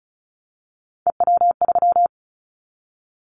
Morse code, heard as a keyed tone, about 36 wpm, 700 Hz. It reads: EW3
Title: Callsign Prefixes for Top 50 DXCC - 35wpm